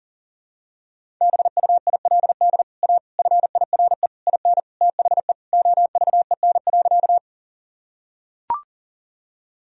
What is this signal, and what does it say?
Morse code, heard as a keyed tone, about 40 wpm, 700 Hz. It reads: BUILD A FIRE IN THE OVEN.
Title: Sentences from Top 1000 Words - Review Part 4 - 40wpm